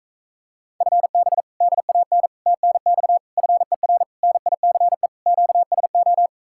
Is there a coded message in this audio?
Yes: Morse code from 0.8 to 6.3 s, heard as a keyed tone, about 45 wpm, 700 Hz. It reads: FB DAN TNX FER NICE QSO